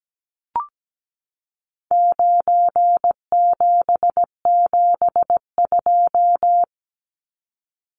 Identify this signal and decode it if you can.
Morse code, heard as a keyed tone, about 17 wpm, 700 Hz. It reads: 9772
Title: Numbers - 4-Digit 17wpm